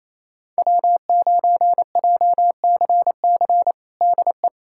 Morse code, heard as a keyed tone, about 27 wpm, 700 Hz. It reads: W9JCC DE
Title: USA Callsign DE Callsign - 5 Characters - 28wpm